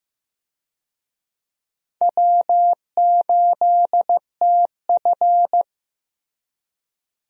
Morse code, heard as a keyed tone, about 15 wpm, 700 Hz. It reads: W8TF
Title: USA Call Signs - 4 Characters - Repeated 3x - 15wpm